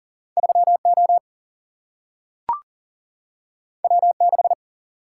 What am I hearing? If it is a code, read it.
Morse code, heard as a keyed tone, about 40 wpm, 700 Hz. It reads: 3O W6
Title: Single-Letter Single-Number 40wpm